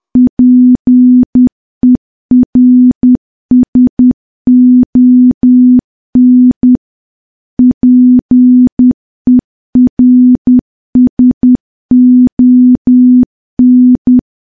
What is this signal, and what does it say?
Morse code, heard as a keyed tone, about 10 wpm, 250 Hz. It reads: PERSON PERSON
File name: morse / morse-training-10WPM